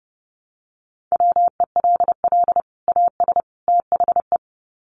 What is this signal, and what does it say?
Morse code, heard as a keyed tone, about 30 wpm, 700 Hz. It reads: WELL AS THE
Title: Sets of 3 Words 30wpm